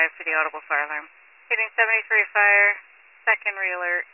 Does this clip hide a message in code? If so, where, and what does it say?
Yes: Emergency Alert System (SAME) tones from 0.7 to 3.6 s, heard as harsh buzzing bursts, with speech.